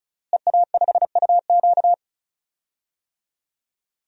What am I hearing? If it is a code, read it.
Morse code, heard as a keyed tone, about 34 wpm, 700 Hz. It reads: EA5UQ
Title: DXCC-100 Call Signs - 5 Characters - Repeated 2x - 35wpm